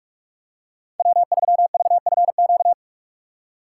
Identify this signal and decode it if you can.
Morse code, heard as a keyed tone, about 46 wpm, 700 Hz. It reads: W3VFQ
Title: USA Call Signs - 5 Characters - 2x Character Spacing - 45wpm